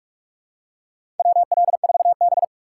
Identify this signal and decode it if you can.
Morse code, heard as a keyed tone, about 44 wpm, 690 Hz. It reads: WL4B
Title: USA Call Signs - 4 Characters - 4x Character Spacing - 45wpm